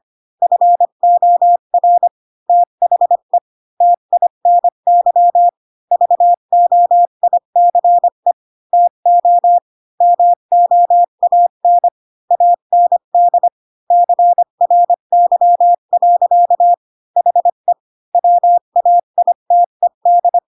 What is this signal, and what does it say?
Morse code, heard as a keyed tone, about 25 wpm, 700 Hz. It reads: FOR THE TINY VOICE TO MOAN AND CRY. HE WAITED
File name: Pinocchio / Pinocchio ch1 20 25 ik1hge